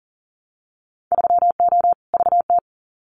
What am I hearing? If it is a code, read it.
Morse code, heard as a keyed tone, about 40 wpm, 700 Hz. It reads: 3O VT
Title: Winter Field Day Exchanges Encore 40wpm